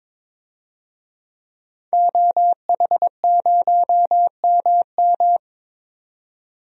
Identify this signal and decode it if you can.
Morse code, heard as a keyed tone, about 22 wpm, 700 Hz. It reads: OH0MM
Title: DXCC-100 Call Signs - 5 Characters - 22wpm